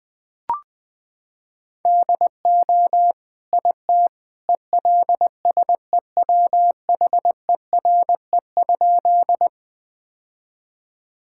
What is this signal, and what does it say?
Morse code, heard as a keyed tone, about 20 wpm, 700 Hz. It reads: DO IT ELSEWHERE?
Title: Sentences from Top 1600 Words 20wpm